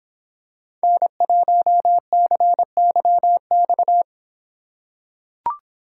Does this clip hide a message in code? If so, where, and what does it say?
Yes: Morse code from 0.8 to 4.0 s, heard as a keyed tone, about 25 wpm, 700 Hz. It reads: N1CYX